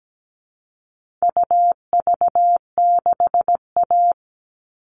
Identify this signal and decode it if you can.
Morse code, heard as a keyed tone, about 17 wpm, 700 Hz. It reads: UV6A
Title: All Cut Numbers - 4-Digit 17wpm